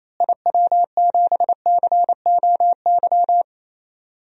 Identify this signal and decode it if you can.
Morse code, heard as a keyed tone, about 28 wpm, 700 Hz. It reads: IW7COY
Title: DXCC-100 Call Signs - 6 Characters - Repeated 2x - 28wpm